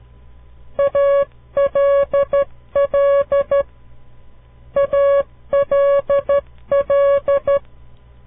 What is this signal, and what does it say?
Morse code, heard as a keyed tone, about 13 wpm, 560 Hz. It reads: ALL ALL
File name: CW Learning Programs / farnsworth / unit 2